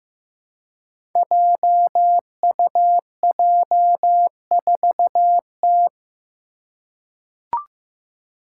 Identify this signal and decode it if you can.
Morse code, heard as a keyed tone, about 15 wpm, 700 Hz. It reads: JUJ4T
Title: Five Challenging Letter-Number Combinations 15wpm